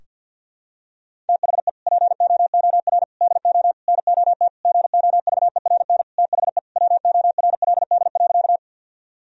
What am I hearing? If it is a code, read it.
Morse code, heard as a keyed tone, about 52 wpm, 700 Hz. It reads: THE POOR DO NOT GOVERN THE WORLD.
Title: Sentences from Top 1000 Words - Review Part 2 - 50wpm